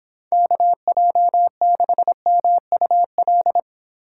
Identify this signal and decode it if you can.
Morse code, heard as a keyed tone, about 31 wpm, 700 Hz. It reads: KJ6MUL